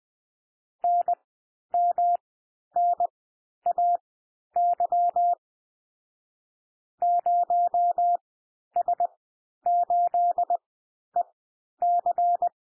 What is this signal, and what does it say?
Morse code, heard as a keyed tone, about 20 wpm, spaced as for 11 wpm, 700 Hz. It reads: NMNAY 0S8EC
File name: cwmp3 / M1 SOUND MIXED 20 wpm